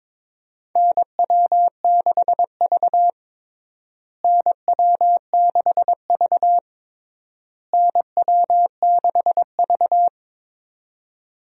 Morse code, heard as a keyed tone, about 21 wpm, 700 Hz. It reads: NW6V NW6V NW6V
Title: USA Call Signs - 4 Characters - Repeated 3x - 22wpm